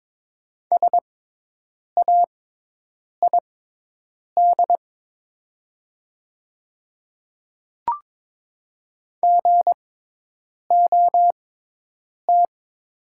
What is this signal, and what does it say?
Morse code, heard as a keyed tone, about 22 wpm, spaced as for 8 wpm, 700 Hz. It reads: SAID GOT
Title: 6x Character Spacing - Top 300 Words - 22wpm